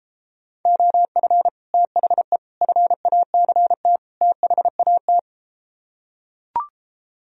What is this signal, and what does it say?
Morse code, heard as a keyed tone, about 33 wpm, 700 Hz. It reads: OF THE FACT THAT